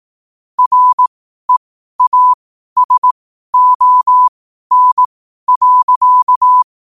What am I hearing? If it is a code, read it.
Morse code, heard as a keyed tone, about 18 wpm, spaced as for 12 wpm, 1000 Hz. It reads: REASON.
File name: qotdfiles / 12wpm